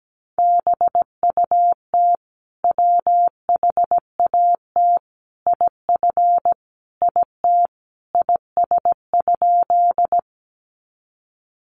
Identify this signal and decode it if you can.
Morse code, heard as a keyed tone, about 17 wpm, 700 Hz. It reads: BUT WHAT IF IT IS?
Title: Sentences from Top 100 Words 17wpm